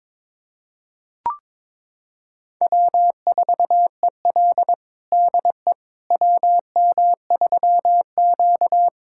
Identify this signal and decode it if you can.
Morse code, heard as a keyed tone, about 22 wpm, 700 Hz. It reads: W4EL DE WM3Q
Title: USA Callsign DE Callsign - 4 Characters - 22wpm